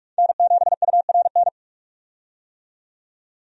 Morse code, heard as a keyed tone, about 46 wpm, 700 Hz. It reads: N7URN
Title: USA Call Signs - 5 Characters - 45wpm